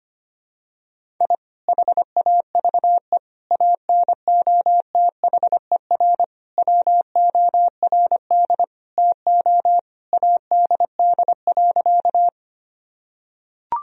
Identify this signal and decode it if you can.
Morse code, heard as a keyed tone, about 25 wpm, 700 Hz. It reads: I HAVE ANOTHER WORD TO ADD.
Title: Sentences from Top 1000 Words - Review Part 1 - 25wpm